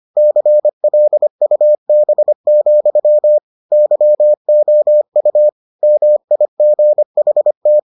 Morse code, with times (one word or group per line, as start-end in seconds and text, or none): 0.2-3.4 CLUB,
3.7-5.5 YOU
5.8-7.8 MIGHT